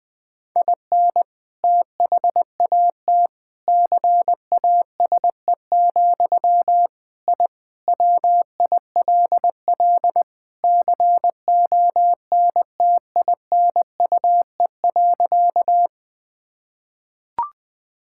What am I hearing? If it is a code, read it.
Morse code, heard as a keyed tone, about 21 wpm, 700 Hz. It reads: IN THAT CASE, I WILL CONTINUE.
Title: Sentences from Top 1000 Words - Review Part 6 - 20wpm